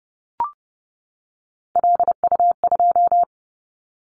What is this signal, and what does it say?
Morse code, heard as a keyed tone, about 30 wpm, 700 Hz. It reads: LU2
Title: Callsign Prefixes for Top 50 DXCC - 30wpm